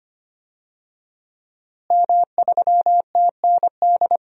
Morse code, heard as a keyed tone, about 25 wpm, 700 Hz. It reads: M3TND
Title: DXCC-100 Call Signs - 5 Characters - Repeated 2x - 25wpm